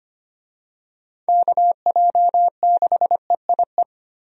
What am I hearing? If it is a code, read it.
Morse code, heard as a keyed tone, about 25 wpm, 700 Hz. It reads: KJ6EIE
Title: USA Call Signs - 6 Characters - 25wpm